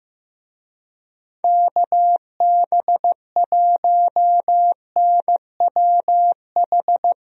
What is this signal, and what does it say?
Morse code, heard as a keyed tone, about 15 wpm, 700 Hz. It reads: KB1NWH
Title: USA Call Signs - 6 Characters - 15wpm